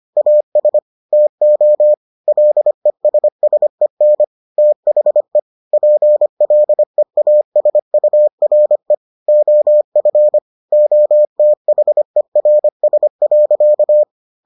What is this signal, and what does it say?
Morse code, heard as a keyed tone, about 25 wpm, 600 Hz. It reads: AS TO LESSEN THE PLEASURE OF OTHERS.